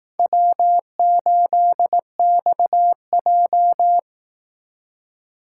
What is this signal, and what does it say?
Morse code, heard as a keyed tone, about 18 wpm, 700 Hz. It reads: W8XJ
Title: Speed Racing - USA 1x2 Call Signs 15wpm